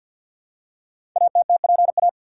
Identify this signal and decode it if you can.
Morse code, heard as a keyed tone, about 50 wpm, 700 Hz. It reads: ATTPA
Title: Code Groups - 5 Letters - 50wpm